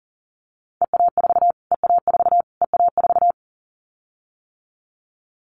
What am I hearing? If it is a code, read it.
Morse code, heard as a keyed tone, about 40 wpm, 700 Hz. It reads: EA4 EA4 EA4
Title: Callsign Prefixes for Top 25 DXCC - Repeated 3x - 40wpm